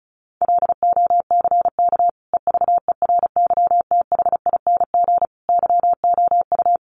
Morse code, heard as a keyed tone, about 34 wpm, 700 Hz. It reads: LOCK EVERYTHING YOU
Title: Sentences from Top 1300 Words 35wpm